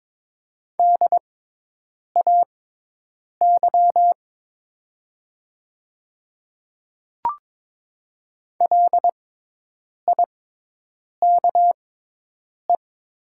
Morse code, heard as a keyed tone, about 22 wpm, spaced as for 8 wpm, 700 Hz. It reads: DAY LIKE
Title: 6x Character Spacing - Top 100 Words - 22wpm